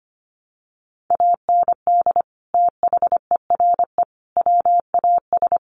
Morse code, heard as a keyed tone, about 25 wpm, 700 Hz. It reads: AND THERE WAS
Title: Sets of 3 Words 25wpm